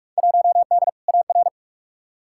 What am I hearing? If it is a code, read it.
Morse code, heard as a keyed tone, about 46 wpm, 700 Hz. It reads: 1D AR
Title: Field Day Exchanges Encore - 45wpm